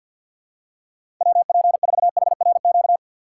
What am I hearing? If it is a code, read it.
Morse code, heard as a keyed tone, about 51 wpm, 700 Hz. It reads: WP4HRQ
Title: USA Call Signs - 6 Characters - 4x Character Spacing - 50wpm